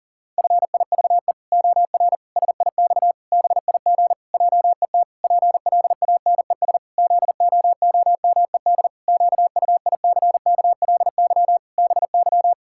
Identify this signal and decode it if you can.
Morse code, heard as a keyed tone, about 40 wpm, 700 Hz. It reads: FIVE OR SIX BIG JET PLANES ZOOMED QUICKLY BY
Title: Pangram Challenge v2 - 40wpm